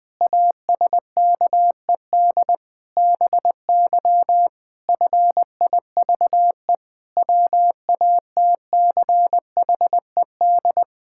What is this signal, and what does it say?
Morse code, heard as a keyed tone, about 21 wpm, 700 Hz. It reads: ASKED BY FIVE WATCHED
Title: Pangram Challenge v2 - 20wpm